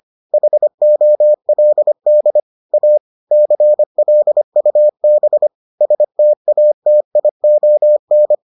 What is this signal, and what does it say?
Morse code, heard as a keyed tone, about 25 wpm, 600 Hz. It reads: HOLD A CLUB STATION